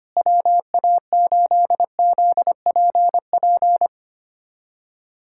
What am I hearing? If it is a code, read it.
Morse code, heard as a keyed tone, about 25 wpm, 700 Hz. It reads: WA8ZPP
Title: USA Callsign DE Callsign - 6 Characters - 25wpm